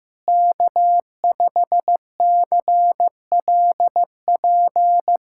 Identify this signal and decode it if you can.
Morse code, heard as a keyed tone, about 15 wpm, 700 Hz. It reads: K5CLP
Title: USA Callsign DE Callsign - 5 Characters - 15wpm